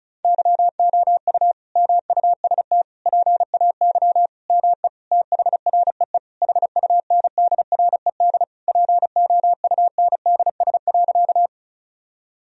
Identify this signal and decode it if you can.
Morse code, heard as a keyed tone, about 34 wpm, 700 Hz. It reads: YOU MUST PAY ME THREE HUNDRED POUNDS.
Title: Sentences from Top 1000 Words - Review Part 2 - 35wpm